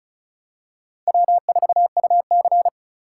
Morse code, heard as a keyed tone, about 36 wpm, 700 Hz. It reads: W4UC
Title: USA Call Signs - 4 Characters - 4x Character Spacing - 35wpm